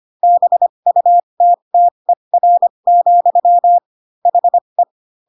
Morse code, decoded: BUTTER, HE

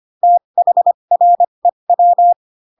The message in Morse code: THREW

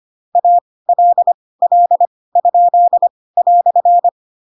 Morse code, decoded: ALL?"

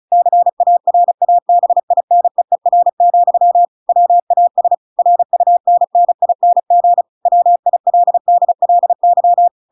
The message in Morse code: CARABINEER, WAS RUNNING WILDLY

